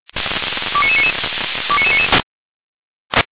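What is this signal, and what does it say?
Five-tone selective call (ZVEI), heard as quick runs of five tones, tones of about 70 ms. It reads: 29999 29999
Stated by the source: ZVEI five-tone selective call received via SDR.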